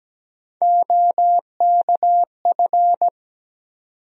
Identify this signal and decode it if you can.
Morse code, heard as a keyed tone, about 17 wpm, 700 Hz. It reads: OKF
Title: USA Call Sign Postfixes - 3 Letters 17wpm